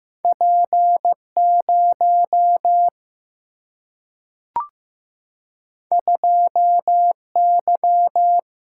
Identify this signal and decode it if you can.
Morse code, heard as a keyed tone, about 15 wpm, 700 Hz. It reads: P0 2Y
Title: Single-Letter Single-Number 15wpm